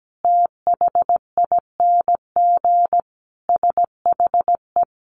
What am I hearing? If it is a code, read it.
Morse code, heard as a keyed tone, about 17 wpm, 700 Hz. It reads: THING SHE
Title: Easy Phrases 17wpm